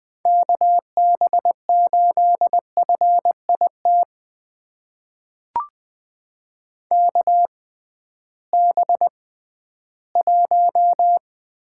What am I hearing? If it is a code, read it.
Morse code, heard as a keyed tone, about 20 wpm, spaced as for 13 wpm, 700 Hz. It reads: KB8FIT KB1
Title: USA Call Signs - 6 Characters - 6x Character Spacing - 20wpm